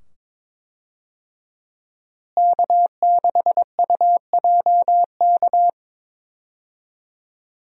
Morse code, heard as a keyed tone, about 22 wpm, 700 Hz. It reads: K6UJK
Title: USA Call Signs - 5 Characters - Repeated 2x - 22wpm